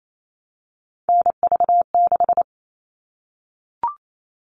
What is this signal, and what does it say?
Morse code, heard as a keyed tone, about 28 wpm, 700 Hz. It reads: NV6
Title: All Cut Numbers - 3-Digit 28wpm